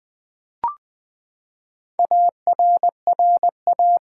Morse code, heard as a keyed tone, about 20 wpm, 700 Hz. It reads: ARRA
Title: Build A Word - 4 Letters At A Time - 11-Letter Words - 20wpm